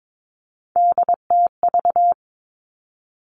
Morse code, heard as a keyed tone, about 22 wpm, 700 Hz. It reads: DTV
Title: Three Challenging Letter-Number Combinations 22wpm